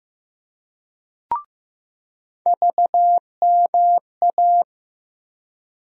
Morse code, heard as a keyed tone, about 15 wpm, 700 Hz. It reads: VMA
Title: USA Call Sign Postfixes - 3 Letters 15wpm